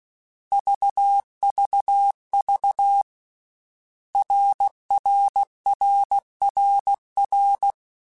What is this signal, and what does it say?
Morse code, heard as a keyed tone, about 16 wpm, 800 Hz. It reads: VVV RRRRR
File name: Basic Letter Groups By7Groups / RKPX / RKPX@13